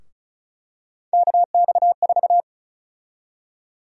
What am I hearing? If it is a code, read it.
Morse code, heard as a keyed tone, about 34 wpm, 700 Hz. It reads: KX4
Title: USA Call Sign Prefixes - 35wpm